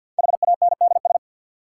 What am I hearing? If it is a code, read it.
Morse code, heard as a keyed tone, about 49 wpm, 700 Hz. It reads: HANDS
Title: Sentences from Top 1000 Words - Review Part 6 - New Word Drill - 50wpm